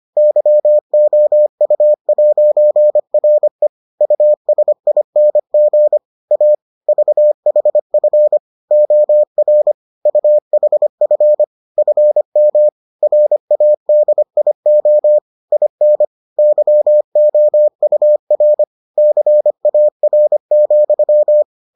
Morse code, heard as a keyed tone, about 25 wpm, 600 Hz. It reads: YOU'RE USING A VHF OR UHF FM RADIO IN YOUR CAR,